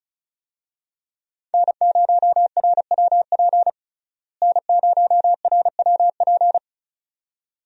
Morse code, heard as a keyed tone, about 36 wpm, 700 Hz. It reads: N0RWP N0RWP
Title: USA Call Signs - 5 Characters - Repeated 2x - 35wpm